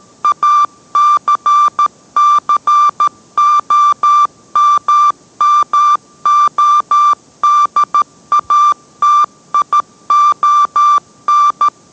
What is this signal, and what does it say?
Morse code, heard as a keyed tone, about 13 wpm, 1180 Hz. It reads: ACCOMMODATION